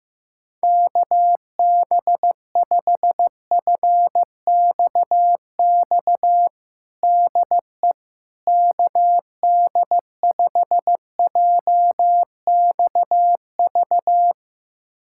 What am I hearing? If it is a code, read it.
Morse code, heard as a keyed tone, about 15 wpm, 700 Hz. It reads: KB5FXX DE KD5JXV
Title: USA Callsign DE Callsign - 6 Characters - 15wpm